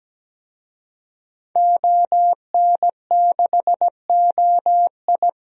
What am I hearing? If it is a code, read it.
Morse code, heard as a keyed tone, about 17 wpm, 700 Hz. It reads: ON6OI
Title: DXCC-100 Call Signs - 5 Characters - Repeated 2x - 17wpm